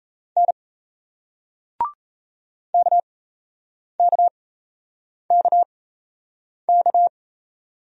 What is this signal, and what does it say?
Morse code, heard as a keyed tone, about 38 wpm, spaced as for 8 wpm, 700 Hz. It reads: N KKKK